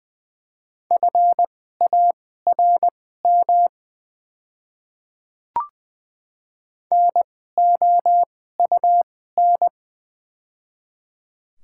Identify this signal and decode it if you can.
Morse code, heard as a keyed tone, about 20 wpm, spaced as for 14 wpm, 690 Hz. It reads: FARM NOUN
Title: 2x Character Spacing - Top 500 Words - 20wpm